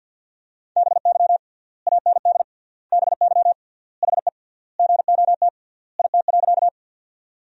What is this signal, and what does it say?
Morse code, heard as a keyed tone, about 50 wpm, 700 Hz. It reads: BY AND BY HE GOT IT.
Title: 3x Word-Spacing - Sentences from Top 300 Words - 50wpm